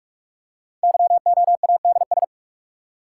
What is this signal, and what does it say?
Morse code, heard as a keyed tone, about 44 wpm, 700 Hz. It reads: YOADS
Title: Code Groups - 5 Letter:Numbers - 45wpm